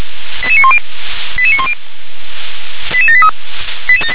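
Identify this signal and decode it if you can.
Five-tone selective call (ZVEI), heard as quick runs of five tones, tones of about 80 ms. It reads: 88910 88910 88783
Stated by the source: ZVEI five-tone selective call 88786 received on an SDR.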